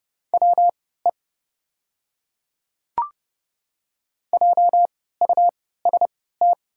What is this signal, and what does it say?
Morse code, heard as a keyed tone, about 30 wpm, spaced as for 17 wpm, 690 Hz. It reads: WE JUST
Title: 3x Character Spacing - Top 500 Words - 30wpm